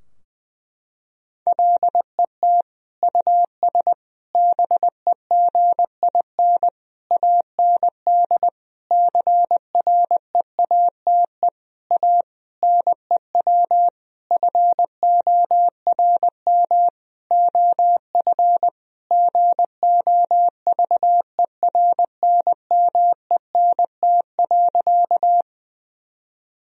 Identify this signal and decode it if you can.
Morse code, heard as a keyed tone, about 20 wpm, 700 Hz. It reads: LET US BEGIN AND CREATE A NEW FORM OF GOVERNMENT.
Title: Sentences from Top 1000 Words - Review Part 3 - 20wpm